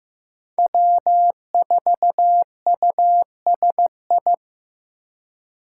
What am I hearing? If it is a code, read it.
Morse code, heard as a keyed tone, about 15 wpm, 700 Hz. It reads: W4USI
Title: K1USN SST QSO Practice - 15wpm